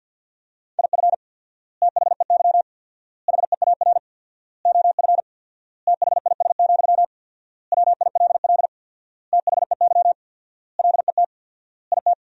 Morse code, heard as a keyed tone, about 50 wpm, 700 Hz. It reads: IF THEY HEAR OF THIS, WILL THEY LET IT
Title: 4x Word-Spacing - Sentences from Top 400 Words - 50wpm